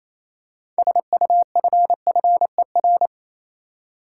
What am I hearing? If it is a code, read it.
Morse code, heard as a keyed tone, about 28 wpm, 700 Hz. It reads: SUFFER
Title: Six Letter Words - 28wpm